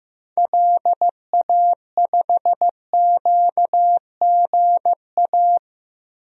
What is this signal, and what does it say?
Morse code, heard as a keyed tone, about 15 wpm, 700 Hz. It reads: LA5QGA
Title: DXCC-100 Call Signs - 6 Characters - Repeated 2x - 15wpm